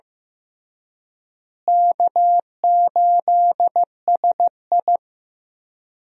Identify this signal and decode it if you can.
Morse code, heard as a keyed tone, about 15 wpm, 700 Hz. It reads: K8SI
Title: USA Call Signs - 4 Characters - Repeated 3x - 15wpm